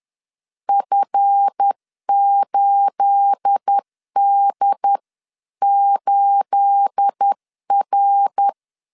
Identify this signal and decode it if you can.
Morse code, heard as a keyed tone, about 10 wpm, 800 Hz. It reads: F8D8R
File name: LEZIONI / 9 28 10wpm 2842